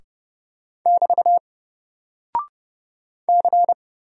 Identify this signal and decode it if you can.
Morse code, heard as a keyed tone, about 30 wpm, 700 Hz. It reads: <BT> C
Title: Single Character (Letter:Number:Punctuation:Prosign) Alt 30wpm